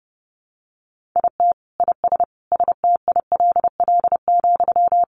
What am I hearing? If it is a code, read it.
Morse code, heard as a keyed tone, about 30 wpm, 700 Hz. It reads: IT IS STILL,
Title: Sentences from Top 1000 Words 30wpm